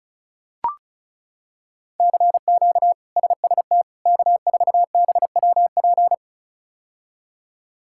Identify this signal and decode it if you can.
Morse code, heard as a keyed tone, about 36 wpm, 700 Hz. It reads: CQ SST K4BWP
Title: K1USN SST QSO Practice - 35wpm